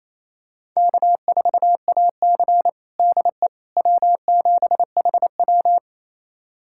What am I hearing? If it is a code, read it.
Morse code, heard as a keyed tone, about 28 wpm, 700 Hz. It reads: K4AC DE W7HW
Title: USA Callsign DE Callsign - 4 Characters - 28wpm